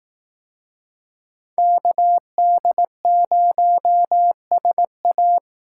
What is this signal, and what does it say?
Morse code, heard as a keyed tone, about 18 wpm, 700 Hz. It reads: KD0SA
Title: Speed Racing - USA 2x2 Call Signs 15wpm